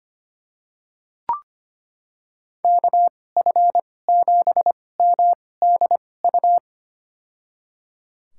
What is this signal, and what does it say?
Morse code, heard as a keyed tone, about 25 wpm, spaced as for 18 wpm, 700 Hz. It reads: KF7MDU
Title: USA Call Signs - 6 Characters - 2x Character Spacing - 25wpm